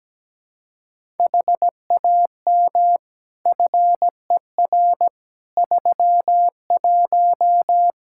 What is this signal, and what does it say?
Morse code, heard as a keyed tone, about 17 wpm, 700 Hz. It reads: HAM FER 31
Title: Practice QSO - Exchange 3 - More Info - 17wpm